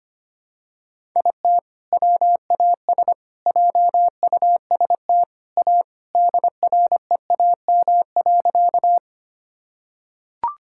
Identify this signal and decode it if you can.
Morse code, heard as a keyed tone, about 25 wpm, 700 Hz. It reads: IT WAS JUST A DREAM.
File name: practice / 25wpm / Sentences with 3-Word Phrases - 25wpm